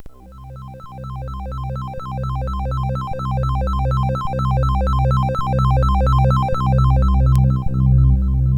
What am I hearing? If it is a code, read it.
There is no code.